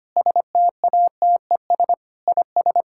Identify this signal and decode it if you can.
Morse code, heard as a keyed tone, about 25 wpm, 700 Hz. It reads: STATES IS
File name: practice / 25wpm / Sentences with 4-Word Phrases - 25wpm